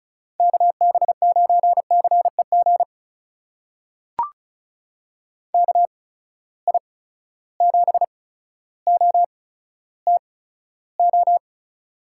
Morse code, heard as a keyed tone, about 34 wpm, spaced as for 15 wpm, 700 Hz. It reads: KB9CEG KI7OTO
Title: USA Call Signs - 6 Characters - 8x Character Spacing - 35wpm